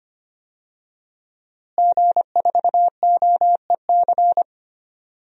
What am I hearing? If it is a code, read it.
Morse code, heard as a keyed tone, about 25 wpm, 700 Hz. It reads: G4OEC
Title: DXCC-100 Call Signs - 5 Characters - Repeated 2x - 25wpm